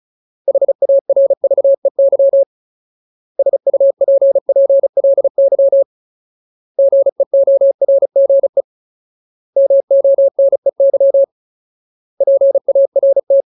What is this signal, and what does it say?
Morse code, heard as a keyed tone, about 36 wpm, 550 Hz. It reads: HARVEY SUPPLY GEORGE MONEY PART
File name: Ultimate Word List / 35WPM Ultimate